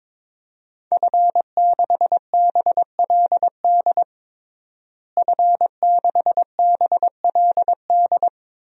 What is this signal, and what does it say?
Morse code, heard as a keyed tone, about 21 wpm, 700 Hz. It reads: F6BLD F6BLD
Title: DXCC-100 Call Signs - 5 Characters - Repeated 2x - 22wpm